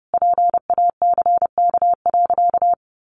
Morse code, heard as a keyed tone, about 30 wpm, 700 Hz. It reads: PACK.
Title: Sentences from Top 1300 Words 30wpm